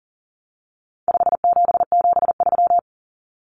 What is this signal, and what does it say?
Morse code, heard as a keyed tone, about 40 wpm, 700 Hz. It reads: 5773
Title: Common Cut Numbers - 4-Digit 40wpm